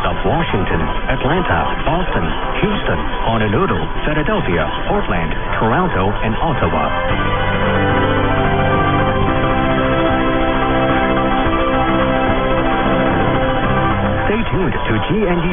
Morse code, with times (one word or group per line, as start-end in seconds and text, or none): none